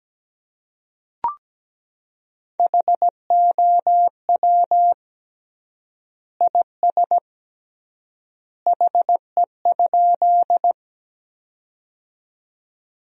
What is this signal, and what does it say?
Morse code, heard as a keyed tone, about 17 wpm, 700 Hz. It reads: HOW IS HE?
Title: 3x Word-Spacing - Sentences from Top 100 Words - 17wpm